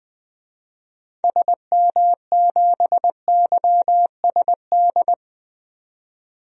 Morse code, heard as a keyed tone, about 20 wpm, 700 Hz. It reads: SM7YSD
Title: DXCC-100 Call Signs - 6 Characters - 20wpm